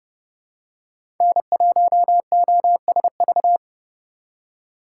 Morse code, heard as a keyed tone, about 31 wpm, 700 Hz. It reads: N1OSV
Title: USA Call Signs - 5 Characters - 3x Character Spacing - 30wpm